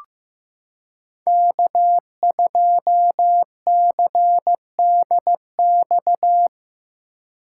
Morse code, heard as a keyed tone, about 15 wpm, 700 Hz. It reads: K2CDX
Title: K1USN SST QSO Practice - 15wpm